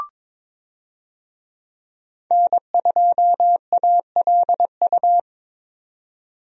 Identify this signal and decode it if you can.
Morse code, heard as a keyed tone, about 22 wpm, 700 Hz. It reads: N2ALU
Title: USA Call Signs - 5 Characters - 22wpm